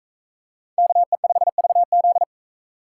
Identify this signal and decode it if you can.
Morse code, heard as a keyed tone, about 42 wpm, 700 Hz. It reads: KE5VZ